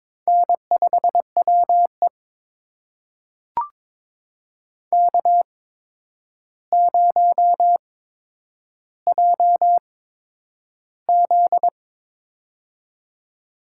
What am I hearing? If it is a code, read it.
Morse code, heard as a keyed tone, about 22 wpm, spaced as for 9 wpm, 700 Hz. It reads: N5WE K0JZ
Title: USA Call Signs - 4 Characters - 8x Character Spacing - 22wpm